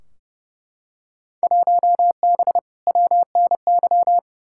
Morse code, heard as a keyed tone, about 31 wpm, 700 Hz. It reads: 1B WNY
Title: Field Day Exchanges - 30wpm